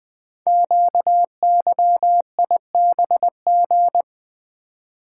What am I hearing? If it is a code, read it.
Morse code, heard as a keyed tone, about 20 wpm, 700 Hz. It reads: QYIBG
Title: Code Groups - 5 Letters - 20wpm